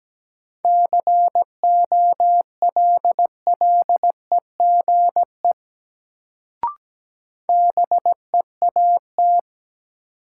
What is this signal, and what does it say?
Morse code, heard as a keyed tone, about 17 wpm, 700 Hz. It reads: COLLEGE BEAT
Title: Sentences from Top 1300 Words - New Word Drill - 17wpm